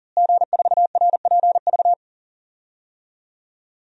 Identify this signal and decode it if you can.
Morse code, heard as a keyed tone, about 40 wpm, 700 Hz. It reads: G4RPV
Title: DXCC-100 Call Signs - 5 Characters - Repeated 3x - 40wpm